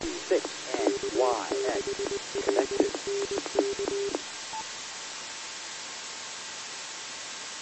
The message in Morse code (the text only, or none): none